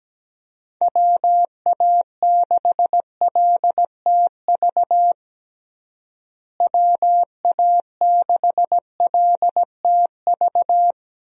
Morse code, heard as a keyed tone, about 17 wpm, 700 Hz. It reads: WA6LTV WA6LTV
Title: USA Call Signs - 6 Characters - Repeated 3x - 17wpm